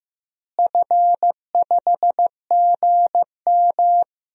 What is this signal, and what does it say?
Morse code, heard as a keyed tone, about 15 wpm, 700 Hz. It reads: F5GM
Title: DXCC-100 Call Signs - 4 Characters - Repeated 2x - 15wpm